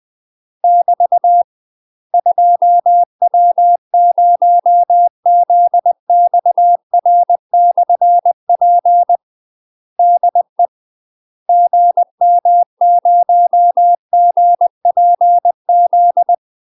Morse code, heard as a keyed tone, about 20 wpm, 700 Hz. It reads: <BT> 2W0ZXR/P DE GM0GPZ
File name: UK MASTER QSOs TRAINING / 20WPM / UK MASTER QSOs 0056